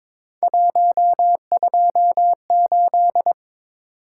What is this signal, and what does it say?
Morse code, heard as a keyed tone, about 21 wpm, 700 Hz. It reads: 128
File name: practice / 22wpm / Three Letter-Number Combinations - 22wpm